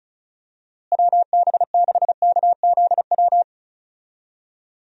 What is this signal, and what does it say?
Morse code, heard as a keyed tone, about 34 wpm, 700 Hz. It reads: WB6KZW
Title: USA Call Signs - 6 Characters - Repeated 3x - 35wpm